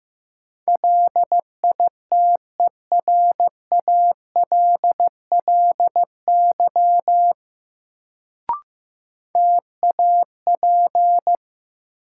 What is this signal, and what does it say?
Morse code, heard as a keyed tone, about 15 wpm, 700 Hz. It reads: LITERALLY TAP
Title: Sentences from Top 1400 Words - New Word Drill - 15wpm